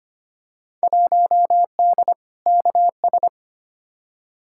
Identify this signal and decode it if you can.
Morse code, heard as a keyed tone, about 25 wpm, 700 Hz. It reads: 1D KS
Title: Field Day Exchanges - 25wpm